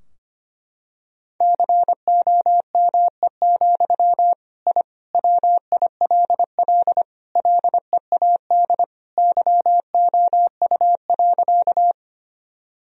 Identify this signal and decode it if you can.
Morse code, heard as a keyed tone, about 25 wpm, 700 Hz. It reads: COME, I WILL LEAD YOU.
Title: Sentences from Top 1000 Words - Review Part 2 - 25wpm